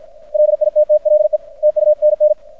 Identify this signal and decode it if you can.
Morse code, heard as a keyed tone, about 53 wpm, 600 Hz. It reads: LITTLE TUNN
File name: archived / CRSnet